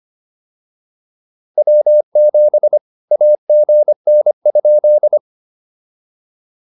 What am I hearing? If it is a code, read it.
Morse code, heard as a keyed tone, about 25 wpm, 600 Hz. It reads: W7 AGN?